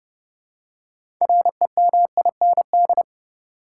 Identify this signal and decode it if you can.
Morse code, heard as a keyed tone, about 29 wpm, 700 Hz. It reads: REMIND
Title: Six Letter Words - 30wpm